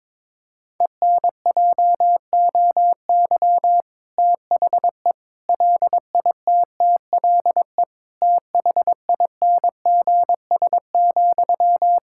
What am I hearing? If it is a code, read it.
Morse code, heard as a keyed tone, about 22 wpm, 700 Hz. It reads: ENJOY THE LITTLE THINGS,
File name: Difficult Sentences with Hints - 22wpm